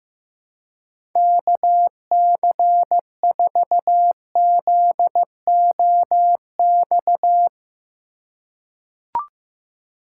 Morse code, heard as a keyed tone, about 15 wpm, 700 Hz. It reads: KC4ZOX
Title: USA Call Signs - 6 Characters - 3x Character Spacing - 15wpm